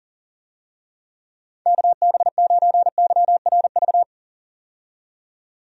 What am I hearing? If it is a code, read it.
Morse code, heard as a keyed tone, about 40 wpm, 700 Hz. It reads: KB9YRV
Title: USA Call Signs - 6 Characters - 40wpm